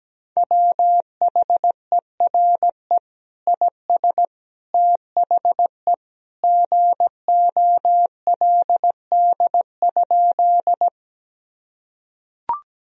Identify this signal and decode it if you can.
Morse code, heard as a keyed tone, about 17 wpm, 700 Hz. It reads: WHERE IS THE GOLD?
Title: Sentences from Top 1000 Words - Review Part 3 - 17wpm